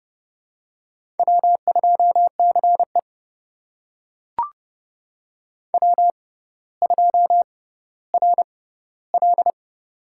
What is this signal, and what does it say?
Morse code, heard as a keyed tone, about 32 wpm, spaced as for 16 wpm, 700 Hz. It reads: W2CE W2RL
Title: USA Call Signs - 4 Characters - 6x Character Spacing - 30wpm